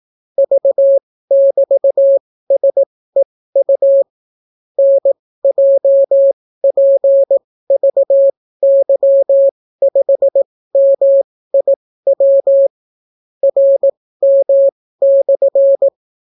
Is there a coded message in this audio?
Yes: Morse code from 0.4 to 15.9 s, heard as a keyed tone, about 18 wpm, spaced as for 14 wpm, 550 Hz. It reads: V<BT>SEU NJPVY5MIW RM/